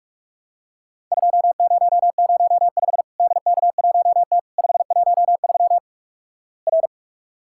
Morse code, heard as a keyed tone, about 45 wpm, 700 Hz. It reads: 2005 DK1T 513 R
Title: Worked All Europe DX Contest - QTC Exchange - 45wpm